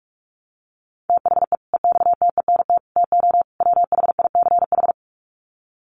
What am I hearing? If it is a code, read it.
Morse code, heard as a keyed tone, about 44 wpm, 700 Hz. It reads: THE EXTENT TO WHICH
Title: Sets of 4 Words 45wpm